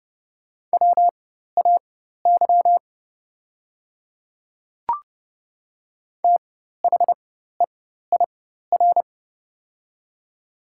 Morse code, heard as a keyed tone, about 30 wpm, spaced as for 14 wpm, 700 Hz. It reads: WAY THEIR
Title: 4x Character Spacing - Top 200 Words - 30wpm